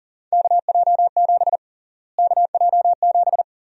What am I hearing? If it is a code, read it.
Morse code, heard as a keyed tone, about 40 wpm, 700 Hz. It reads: KJ7 KJ7
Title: USA Call Sign Prefixes - Repeated 2x - 40wpm